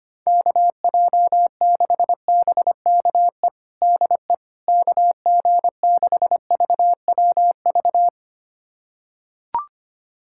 Morse code, heard as a keyed tone, about 25 wpm, 700 Hz. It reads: KJ6BKE DE KG6VWV
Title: USA Callsign DE Callsign - 6 Characters - 25wpm